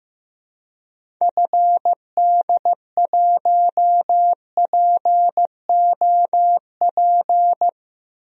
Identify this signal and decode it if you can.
Morse code, heard as a keyed tone, about 15 wpm, 700 Hz. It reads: FD1POP
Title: DXCC-100 Call Signs - 6 Characters - Repeated 2x - 15wpm